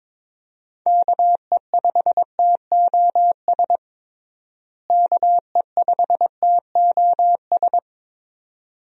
Morse code, heard as a keyed tone, about 22 wpm, 700 Hz. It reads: KE5TOS KE5TOS
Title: USA Call Signs - 6 Characters - Repeated 2x - 22wpm